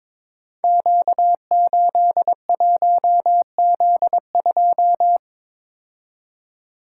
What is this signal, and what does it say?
Morse code, heard as a keyed tone, about 22 wpm, 700 Hz. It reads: Q81Z2
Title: Code Groups - 5 Letter:Numbers - 22wpm